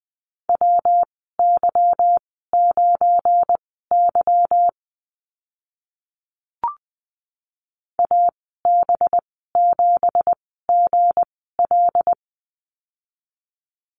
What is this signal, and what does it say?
Morse code, heard as a keyed tone, about 20 wpm, spaced as for 14 wpm, 700 Hz. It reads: WY9Y AB7GL
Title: 193 Lesson 31g - Letter v - Call Signs 20wpm